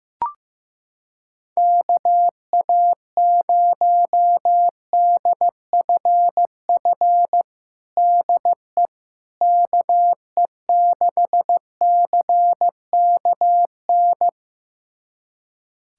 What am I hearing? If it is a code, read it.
Morse code, heard as a keyed tone, about 15 wpm, 700 Hz. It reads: KA0DFF DE KE6CKN
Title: USA Callsign DE Callsign 15wpm